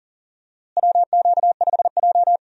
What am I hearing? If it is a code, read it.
Morse code, heard as a keyed tone, about 40 wpm, 700 Hz. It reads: WQ5J
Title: USA Call Signs - 4 Characters - Repeated 2x - 40wpm